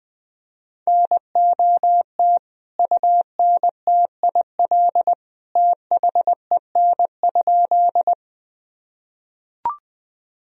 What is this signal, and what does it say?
Morse code, heard as a keyed tone, about 20 wpm, 700 Hz. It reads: NOT UNTIL THEN?
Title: Sentences from Top 1000 Words - Review Part 1 - 20wpm